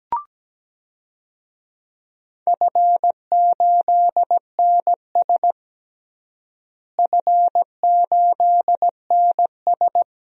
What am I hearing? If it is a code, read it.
Morse code, heard as a keyed tone, about 17 wpm, 700 Hz. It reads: F8NS F8NS
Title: DXCC-100 Call Signs - 4 Characters - Repeated 2x - 17wpm